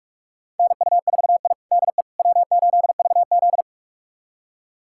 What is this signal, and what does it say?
Morse code, heard as a keyed tone, about 41 wpm, 700 Hz. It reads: NU4I DE W8VZ
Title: USA Callsign DE Callsign - 4 Characters - 45wpm